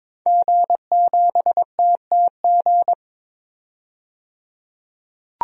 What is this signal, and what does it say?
Morse code, heard as a keyed tone, about 22 wpm, 700 Hz. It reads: G7TTG
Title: DXCC-100 Call Signs - 5 Characters - 22wpm